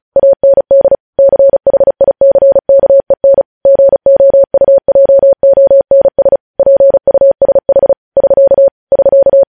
Morse code, heard as a keyed tone, about 36 wpm, 550 Hz. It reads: AND CHICKEN GOUJONS PUSH <SK> <SK>
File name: podcast / News headlines-1x-35x35